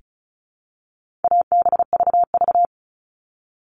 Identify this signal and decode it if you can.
Morse code, heard as a keyed tone, about 34 wpm, 700 Hz. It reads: ABVV
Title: All Cut Numbers - 4-Digit 35wpm